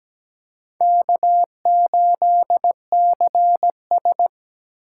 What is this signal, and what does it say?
Morse code, heard as a keyed tone, about 20 wpm, 700 Hz. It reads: K8CS
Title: Speed Racing - USA 1x2 Call Signs 17wpm